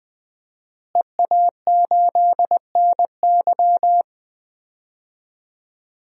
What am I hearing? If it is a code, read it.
Morse code, heard as a keyed tone, about 20 wpm, 700 Hz. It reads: EA8NY
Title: DXCC-100 Call Signs - 5 Characters - Repeated 2x - 20wpm